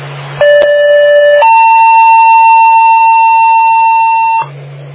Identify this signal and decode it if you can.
Two-tone page, heard as a one-second tone, then a three-second tone, about 580 then 900 Hz. It is heard from 0.4 to 4.4 s.